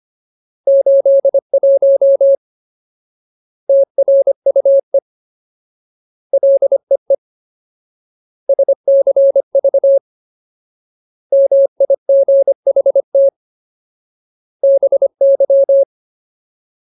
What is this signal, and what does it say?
Morse code, heard as a keyed tone, about 25 wpm, 550 Hz. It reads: 81 TRUE LEE SCV MIGHT BY
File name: Ultimate Word List / 25WPM Ultimate